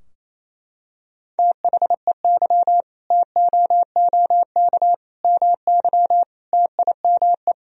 Morse code, heard as a keyed tone, about 28 wpm, 700 Hz. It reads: THEY TOOK MY TIME
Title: Sentences from Top 1000 Words - Review Part 2 - 28wpm